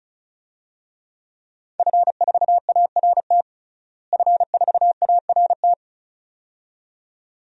Morse code, heard as a keyed tone, about 34 wpm, 700 Hz. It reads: F4ART F4ART
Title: DXCC-100 Call Signs - 5 Characters - Repeated 2x - 35wpm